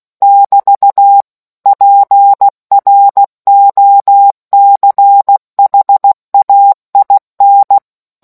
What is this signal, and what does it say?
Morse code, heard as a keyed tone, about 16 wpm, 800 Hz. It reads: <BT> PROCHAIN